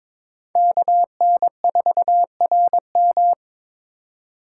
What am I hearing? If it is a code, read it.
Morse code, heard as a keyed tone, about 22 wpm, 700 Hz. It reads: KN4RM
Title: USA Call Signs - 4x Character Spacing - 22wpm